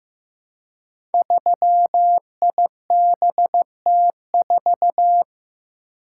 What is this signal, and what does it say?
Morse code, heard as a keyed tone, about 15 wpm, 700 Hz. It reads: 3IBT4
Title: Code Groups - 5 Letter:Numbers - 15wpm